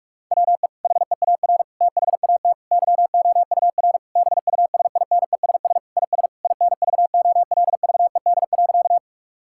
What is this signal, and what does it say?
Morse code, heard as a keyed tone, about 46 wpm, 700 Hz. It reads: WE HEAR THAT YOUR BUSINESS IS INVOLVED.
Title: Sentences from Top 1000 Words - Review Part 5 - 45wpm